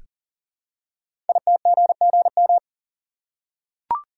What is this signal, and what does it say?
Morse code, heard as a keyed tone, about 41 wpm, 700 Hz. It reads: ITGGM
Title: Code Groups - 5 Letters - 40wpm